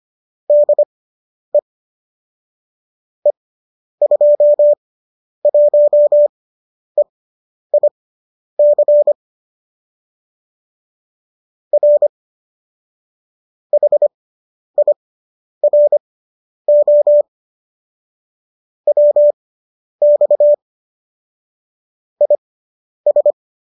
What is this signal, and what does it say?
Morse code, heard as a keyed tone, about 25 wpm, spaced as for 10 wpm, 590 Hz. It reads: DE E21EIC R HIRO WX IS